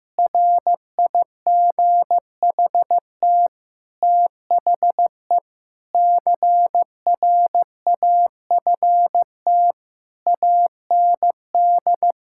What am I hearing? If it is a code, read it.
Morse code, heard as a keyed tone, about 15 wpm, 700 Hz. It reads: RIGHT THE CRAFT AND
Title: Sentences from Top 1000 Words - Review Part 6 - 15wpm